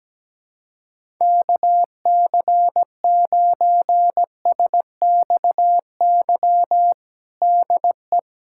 Morse code, heard as a keyed tone, about 17 wpm, 700 Hz. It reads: KC9SXY DE
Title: USA Callsign DE Callsign - 6 Characters - 17wpm